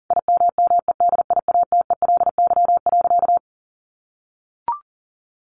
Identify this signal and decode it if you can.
Morse code, heard as a keyed tone, about 37 wpm, 700 Hz. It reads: IMMEDIATELY.
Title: Sentences from Top 1100 Words 40wpm